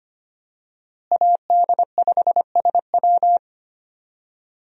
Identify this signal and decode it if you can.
Morse code, heard as a keyed tone, about 25 wpm, 700 Hz. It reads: AD5SW
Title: USA Call Signs - 5 Characters - Repeated 3x - 25wpm